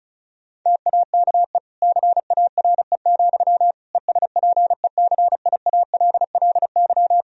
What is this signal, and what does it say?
Morse code, heard as a keyed tone, about 34 wpm, 700 Hz. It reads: TAKE CARE, ESPECIALLY
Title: Sentences from Top 1000 Words - Review Part 4 - 35wpm